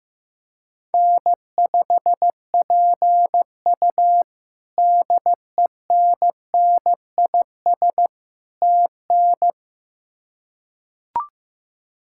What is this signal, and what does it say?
Morse code, heard as a keyed tone, about 16 wpm, 700 Hz. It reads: N5PU DENNIS TN
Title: K1USN SST QSO Practice - 15wpm